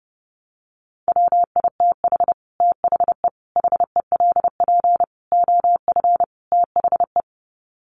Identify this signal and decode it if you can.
Morse code, heard as a keyed tone, about 30 wpm, 700 Hz. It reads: WITH THE HELP OF THE
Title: Sets of 5 Words Encore 30wpm